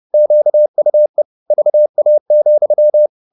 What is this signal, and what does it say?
Morse code, heard as a keyed tone, about 30 wpm, 600 Hz. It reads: QUE VA,
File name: the old man and the sea / the old man and the sea 1 30wpm